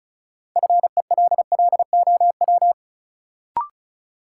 Morse code, heard as a keyed tone, about 36 wpm, 700 Hz. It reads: FELLOW
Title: Sets of 5 Words Encore - New Word Drill - 35wpm